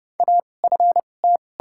Morse code, heard as a keyed tone, about 29 wpm, 700 Hz. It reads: A F T
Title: 2x Character Spacing - Top 200 Words - 30wpm